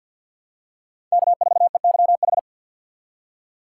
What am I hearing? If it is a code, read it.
Morse code, heard as a keyed tone, about 47 wpm, 700 Hz. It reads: K4EYH
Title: USA Call Signs - 4x Character Spacing - 50wpm